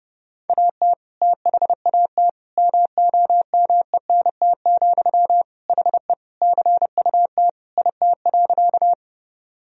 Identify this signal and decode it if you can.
Morse code, heard as a keyed tone, about 32 wpm, 700 Hz. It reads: AT THAT MOMENT, HE CUT IT.
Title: Sentences from Top 1000 Words - Review Part 1 - 30wpm